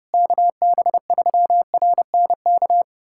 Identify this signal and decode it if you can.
Morse code, heard as a keyed tone, about 30 wpm, 700 Hz. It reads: KB3RNK
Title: USA Callsign DE Callsign 30wpm